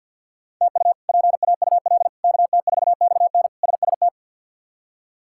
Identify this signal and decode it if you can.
Morse code, heard as a keyed tone, about 49 wpm, 690 Hz. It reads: TU PAUL KT4XN SST
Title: K1USN SST QSO Practice - 50wpm